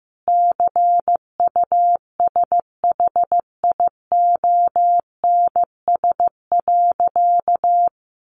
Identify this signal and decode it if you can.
Morse code, heard as a keyed tone, about 15 wpm, 700 Hz. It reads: CUSHIONS.
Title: A Sound Of Thunder 15wpm